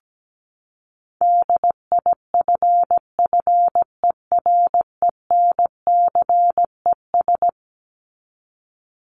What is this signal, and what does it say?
Morse code, heard as a keyed tone, about 17 wpm, 700 Hz. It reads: DIFFERENCES
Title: ES:S Suffix - Top 100 Words with Base Words 17wpm